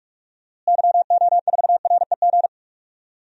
Morse code, heard as a keyed tone, about 43 wpm, 700 Hz. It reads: YO4REG
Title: DXCC-100 Call Signs - 6 Characters - Repeated 3x - 45wpm